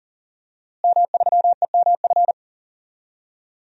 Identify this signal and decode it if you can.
Morse code, heard as a keyed tone, about 37 wpm, 700 Hz. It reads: M3EMF
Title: DXCC-100 Call Signs - 5 Characters - Repeated 3x - 40wpm